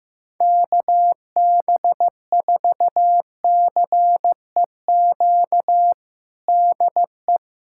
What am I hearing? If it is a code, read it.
Morse code, heard as a keyed tone, about 15 wpm, 700 Hz. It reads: KB4CEQ DE
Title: USA Callsign DE Callsign 15wpm